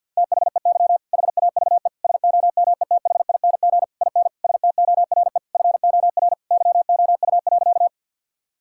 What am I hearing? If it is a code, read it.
Morse code, heard as a keyed tone, about 50 wpm, 700 Hz. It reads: THEY HAVE SOMETHING IN STORE FOR YOU.
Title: Sentences from Top 1000 Words - Review Part 5 - 50wpm